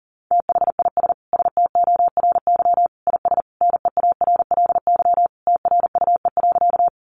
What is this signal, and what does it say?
Morse code, heard as a keyed tone, about 40 wpm, 700 Hz. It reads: THIS STORY IS NEARLY TRUE.
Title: Sentences from Top 1300 Words 40wpm